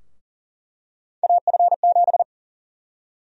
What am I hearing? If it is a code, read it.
Morse code, heard as a keyed tone, about 40 wpm, 700 Hz. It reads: AF7
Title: USA Call Sign Prefixes - Repeated 2x - 40wpm